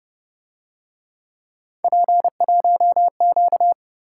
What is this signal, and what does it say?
Morse code, heard as a keyed tone, about 29 wpm, 700 Hz. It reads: P1Q
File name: Speed-Racing - Three Letter-Number Combinations - 25wpm